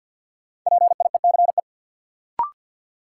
Morse code, heard as a keyed tone, about 52 wpm, 700 Hz. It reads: PIECE
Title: Sentences from Top 400 Words - New Word Drill - 50wpm